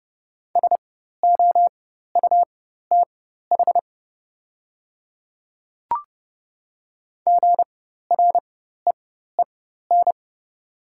Morse code, heard as a keyed tone, about 30 wpm, spaced as for 14 wpm, 700 Hz. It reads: SOUTH GREEN
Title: 4x Character Spacing - Top 500 Words - 30wpm